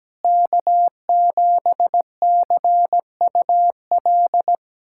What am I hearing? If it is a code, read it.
Morse code, heard as a keyed tone, about 17 wpm, 700 Hz. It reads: K7CUL
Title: USA Callsign DE Callsign 17wpm